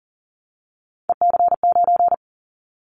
Morse code, heard as a keyed tone, about 40 wpm, 700 Hz. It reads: EC9
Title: Callsign Prefixes for Top 100 DXCC - 40wpm